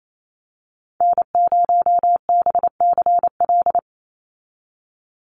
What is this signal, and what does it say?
Morse code, heard as a keyed tone, about 28 wpm, 700 Hz. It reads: N0BCL
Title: USA Call Signs 28wpm